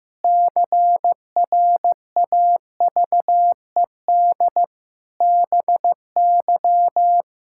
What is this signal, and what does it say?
Morse code, heard as a keyed tone, about 15 wpm, 700 Hz. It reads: CRAVED BY
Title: Pangram Challenge v2 - 15wpm